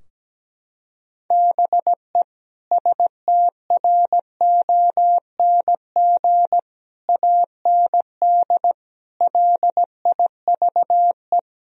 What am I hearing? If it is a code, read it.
Morse code, heard as a keyed tone, about 17 wpm, 700 Hz. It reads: BE STRONG AND LIVE
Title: Sentences from Top 1000 Words - Review Part 3 - 17wpm